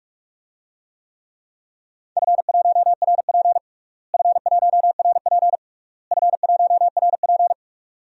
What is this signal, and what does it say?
Morse code, heard as a keyed tone, about 46 wpm, 700 Hz. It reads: F1RP F1RP F1RP
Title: DXCC-100 Call Signs - 4 Characters - Repeated 3x - 45wpm